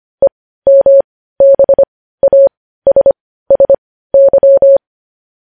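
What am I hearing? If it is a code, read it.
Morse code, heard as a keyed tone, about 25 wpm, spaced as for 15 wpm, 550 Hz. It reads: EMBASSY
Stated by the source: News headlines sent in Morse code at 15 wpm.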